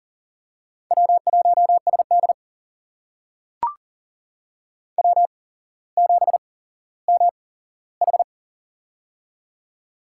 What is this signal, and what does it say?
Morse code, heard as a keyed tone, about 39 wpm, spaced as for 17 wpm, 700 Hz. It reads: W1SD W7MH
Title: USA Call Signs - 4 Characters - 8x Character Spacing - 40wpm